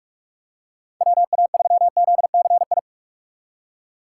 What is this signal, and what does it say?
Morse code, heard as a keyed tone, about 45 wpm, 700 Hz. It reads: WA3ZCI
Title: USA Call Signs - 6x Character Spacing - 45wpm